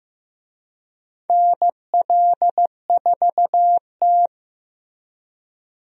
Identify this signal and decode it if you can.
Morse code, heard as a keyed tone, about 15 wpm, 700 Hz. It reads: NL4T
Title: USA Call Signs - 4 Characters - Repeated 3x - 15wpm